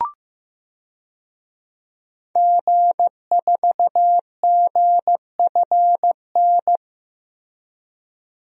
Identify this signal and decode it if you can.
Morse code, heard as a keyed tone, about 15 wpm, 700 Hz. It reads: G4GFN
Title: DXCC-100 Call Signs - 5 Characters - Repeated 3x - 15wpm